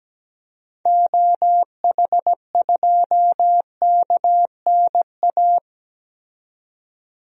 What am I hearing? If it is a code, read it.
Morse code, heard as a keyed tone, about 17 wpm, 700 Hz. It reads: OH2KNA
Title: DXCC-100 Call Signs - 6 Characters - 17wpm